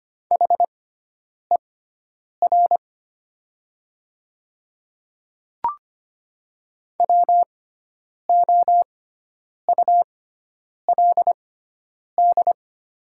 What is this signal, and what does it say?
Morse code, heard as a keyed tone, about 25 wpm, spaced as for 9 wpm, 700 Hz. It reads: HER WOULD
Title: 6x Character Spacing - Top 100 Words - 25wpm